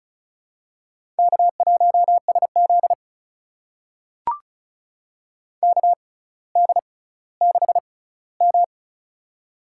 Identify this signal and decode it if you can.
Morse code, heard as a keyed tone, about 36 wpm, spaced as for 18 wpm, 700 Hz. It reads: K1SZ KD6M
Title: USA Call Signs - 4 Characters - 6x Character Spacing - 35wpm